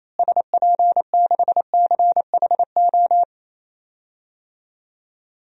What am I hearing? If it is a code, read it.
Morse code, heard as a keyed tone, about 28 wpm, 700 Hz. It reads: SP6CHO
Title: DXCC-100 Call Signs - 6 Characters - Repeated 3x - 28wpm